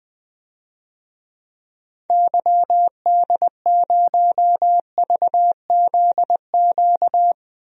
Morse code, heard as a keyed tone, about 20 wpm, 700 Hz. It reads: YD0VZQ
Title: DXCC-100 Call Signs - 6 Characters - 20wpm